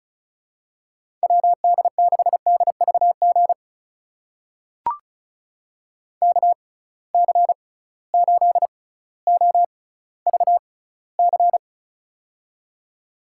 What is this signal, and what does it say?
Morse code, heard as a keyed tone, about 34 wpm, spaced as for 18 wpm, 700 Hz. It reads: WD6DVG KC8OVC
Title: USA Call Signs - 6 Characters - 6x Character Spacing - 35wpm